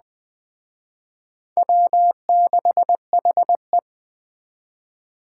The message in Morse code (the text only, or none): W6HE